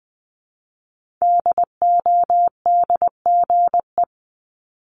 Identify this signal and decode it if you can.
Morse code, heard as a keyed tone, about 20 wpm, 700 Hz. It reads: DODGE
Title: Multi-letter Phonemes Repeated 3x 20wpm